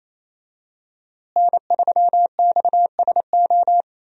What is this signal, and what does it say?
Morse code, heard as a keyed tone, about 27 wpm, 700 Hz. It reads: N3XSO
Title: USA Call Signs - 5 Characters - Repeated 3x - 28wpm